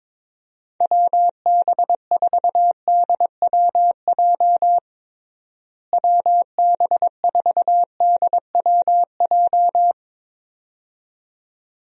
Morse code, heard as a keyed tone, about 21 wpm, 700 Hz. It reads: WB4DWJ WB4DWJ
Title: USA Call Signs - 6 Characters - Repeated 2x - 22wpm